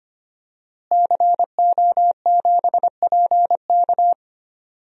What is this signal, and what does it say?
Morse code, heard as a keyed tone, about 25 wpm, 700 Hz. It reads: CO7PK
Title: DXCC-100 Call Signs - 5 Characters - Repeated 2x - 25wpm